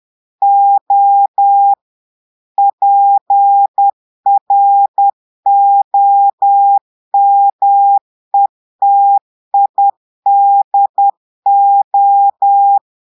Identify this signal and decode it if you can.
Morse code, heard as a keyed tone, about 10 wpm, 800 Hz. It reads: O PROMETIDO
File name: Portugais - texte en clair 1 - vitesse - 10 mots minute